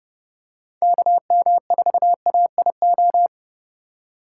Morse code, heard as a keyed tone, about 32 wpm, 700 Hz. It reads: KM4AIO
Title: USA Call Signs - 6x Character Spacing - 30wpm